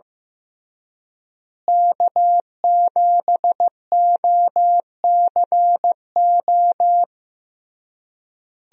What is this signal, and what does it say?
Morse code, heard as a keyed tone, about 16 wpm, 700 Hz. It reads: K7OCO
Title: USA Call Signs - 5 Characters - Repeated 3x - 15wpm